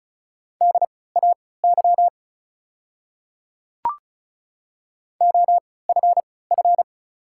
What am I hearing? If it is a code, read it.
Morse code, heard as a keyed tone, about 35 wpm, spaced as for 20 wpm, 700 Hz. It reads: DAY OFF
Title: 3x Character Spacing - Top 200 Words - 35wpm